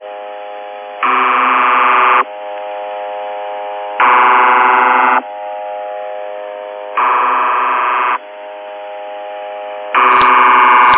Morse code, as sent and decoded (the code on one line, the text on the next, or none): none